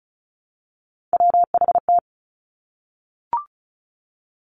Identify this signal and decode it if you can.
Morse code, heard as a keyed tone, about 36 wpm, 700 Hz. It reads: WHT
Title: Three Challenging Letter-Number Combinations 35wpm